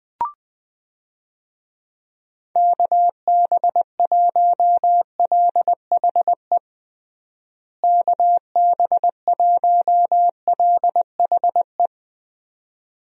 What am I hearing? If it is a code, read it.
Morse code, heard as a keyed tone, about 21 wpm, 700 Hz. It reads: KB1LHE KB1LHE
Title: USA Call Signs - 6 Characters - Repeated 2x - 20wpm